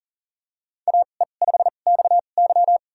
Morse code, heard as a keyed tone, about 40 wpm, spaced as for 29 wpm, 700 Hz. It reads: AE5XY
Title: USA Call Signs - 5 Characters - 2x Character Spacing - 40wpm